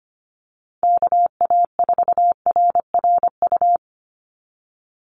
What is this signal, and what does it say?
Morse code, heard as a keyed tone, about 25 wpm, 700 Hz. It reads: KA4RRU
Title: Call Signs Repeated 3x - 25wpm